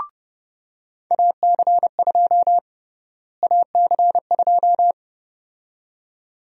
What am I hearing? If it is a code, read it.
Morse code, heard as a keyed tone, about 32 wpm, 700 Hz. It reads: AC2 AC2
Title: USA Call Sign Prefixes - Repeated 2x - 30wpm